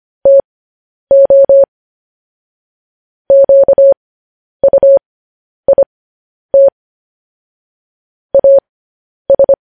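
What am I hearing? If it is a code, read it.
Morse code, heard as a keyed tone, about 25 wpm, spaced as for 10 wpm, 550 Hz. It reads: TO QUIT AS
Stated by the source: Morse code news headlines sent as a reading transcription, preceded by vvv.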